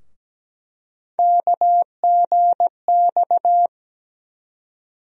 Morse code, heard as a keyed tone, about 17 wpm, 700 Hz. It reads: KGX
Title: USA Call Sign Postfixes - 3 Letters 17wpm